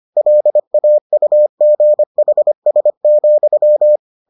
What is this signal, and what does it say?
Morse code, heard as a keyed tone, about 25 wpm, 600 Hz. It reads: LAUGHS,